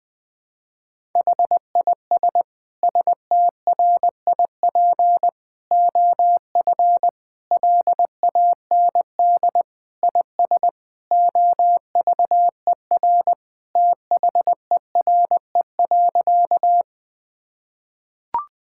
Morse code, heard as a keyed tone, about 20 wpm, 700 Hz. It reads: HIS STRIP OF LAND IS OVER THERE.
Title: Sentences from Top 1600 Words 20wpm